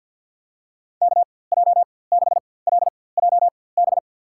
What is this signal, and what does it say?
Morse code, heard as a keyed tone, about 49 wpm, spaced as for 23 wpm, 700 Hz. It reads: KJ6LJB
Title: USA Call Signs - 6 Characters - 4x Character Spacing - 50wpm